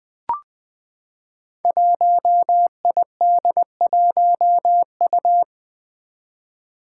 Morse code, heard as a keyed tone, about 20 wpm, 700 Hz. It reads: 1ID1U
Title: Five Challenging Letter-Number Combinations 20wpm